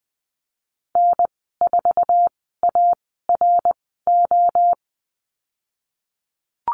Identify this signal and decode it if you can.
Morse code, heard as a keyed tone, about 20 wpm, 700 Hz. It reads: N 4 A R O
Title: 062 Lesson 11g - Letter h - Call Signs 20wpm